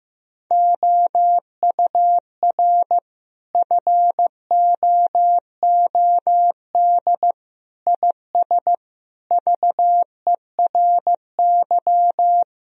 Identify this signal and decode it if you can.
Morse code, heard as a keyed tone, about 15 wpm, 700 Hz. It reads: OUR FOOD IS VERY
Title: Sentences from Top 1000 Words - Review Part 1 - 15wpm